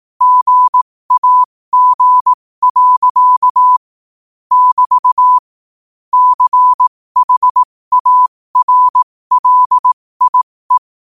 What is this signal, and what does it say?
Morse code, heard as a keyed tone, about 18 wpm, spaced as for 15 wpm, 1000 Hz. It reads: GAG. <BT> CHARLIE
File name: qotdfiles / 15wpm